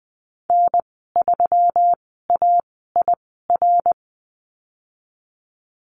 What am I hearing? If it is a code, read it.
Morse code, heard as a keyed tone, about 20 wpm, spaced as for 15 wpm, 700 Hz. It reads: N3AIR
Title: 147 Lesson 24g - Letter y - Call Signs 20wpm